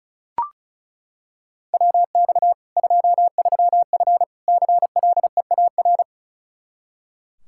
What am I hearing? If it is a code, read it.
Morse code, heard as a keyed tone, about 36 wpm, 700 Hz. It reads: WX 23F CLEAR
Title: K1USN SST QSO Practice 35wpm